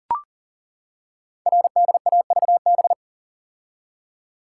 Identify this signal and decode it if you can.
Morse code, heard as a keyed tone, about 42 wpm, 700 Hz. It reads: RDAVB
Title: Code Groups - 5 Letters - 40wpm